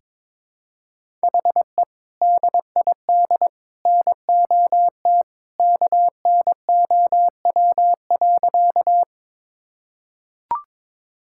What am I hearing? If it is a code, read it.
Morse code, heard as a keyed tone, about 22 wpm, 700 Hz. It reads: HE DID NOT KNOW.
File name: practice / 22wpm / Sentences with 2-Word Phrases - 22wpm